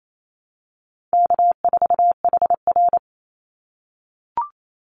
Morse code, heard as a keyed tone, about 28 wpm, 700 Hz. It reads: K4HR
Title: Easy Call Signs - 28wpm